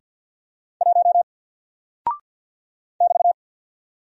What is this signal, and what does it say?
Morse code, heard as a keyed tone, about 48 wpm, 700 Hz. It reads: 1 <BT>
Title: Single Character (Letter:Number:Punctuation:Prosign) Alt 50wpm